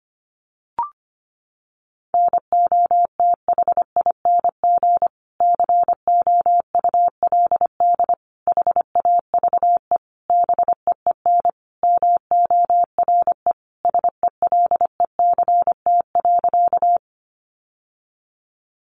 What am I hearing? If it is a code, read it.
Morse code, heard as a keyed tone, about 25 wpm, 700 Hz. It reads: NOTHING COULD HAVE BEEN MORE SELECT.
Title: Sentences from Top 1300 Words 25wpm